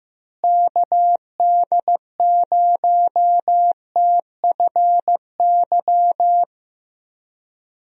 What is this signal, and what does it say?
Morse code, heard as a keyed tone, about 15 wpm, 700 Hz. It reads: KD0TFY
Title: USA Call Signs - 6 Characters - Repeated 2x - 15wpm